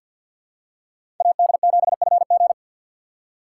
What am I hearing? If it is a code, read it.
Morse code, heard as a keyed tone, about 53 wpm, 700 Hz. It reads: AD7FG
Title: USA Call Signs - 5 Characters - 2x Character Spacing - 50wpm